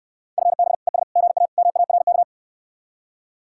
Morse code, heard as a keyed tone, about 69 wpm, 700 Hz. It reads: FB ES GA MARK